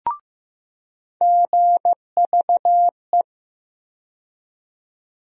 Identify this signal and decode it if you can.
Morse code, heard as a keyed tone, about 15 wpm, 700 Hz. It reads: GVE
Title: USA Call Sign Postfixes - 3 Letters 15wpm